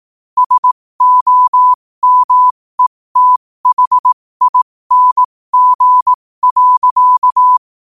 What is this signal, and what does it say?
Morse code, heard as a keyed tone, about 18 wpm, spaced as for 15 wpm, 1000 Hz. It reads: SOMETHING.
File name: qotdfiles / 15wpm